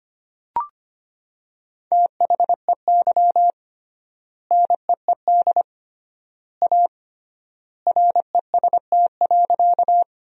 Morse code, heard as a keyed tone, about 25 wpm, 700 Hz. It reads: THEY NEED A REST.
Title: 3x Word-Spacing - Sentences from Top 500 Words - 25wpm